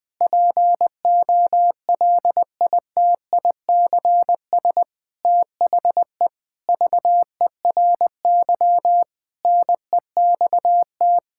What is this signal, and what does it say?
Morse code, heard as a keyed tone, about 19 wpm, 700 Hz. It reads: POLITICS THE VERY NEXT
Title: Sentences from Top 1000 Words - Review Part 6 - 20wpm